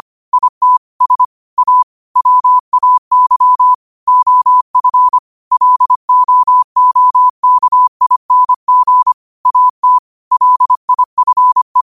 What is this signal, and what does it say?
Morse code, heard as a keyed tone, about 25 wpm, 1000 Hz. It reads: IT S A WAY OF LOOKING AT LIFE